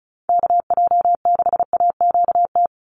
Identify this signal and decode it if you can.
Morse code, heard as a keyed tone, about 34 wpm, 700 Hz. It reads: KJ6AQT
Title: USA Call Signs 35wpm